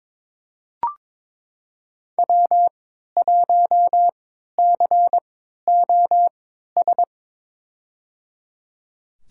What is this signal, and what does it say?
Morse code, heard as a keyed tone, about 22 wpm, spaced as for 12 wpm, 700 Hz. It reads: W1COS
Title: USA Call Signs - 5 Characters - 3x Character Spacing - 22wpm